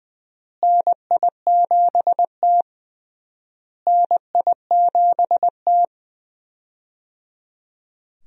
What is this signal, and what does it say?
Morse code, heard as a keyed tone, about 20 wpm, 700 Hz. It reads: NI7T NI7T
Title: USA Call Signs - 4 Characters - Repeated 3x - 20wpm